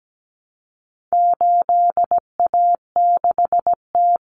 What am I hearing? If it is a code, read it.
Morse code, heard as a keyed tone, about 17 wpm, 700 Hz. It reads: 8A6T
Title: Common Cut Numbers - 4-Digit 17wpm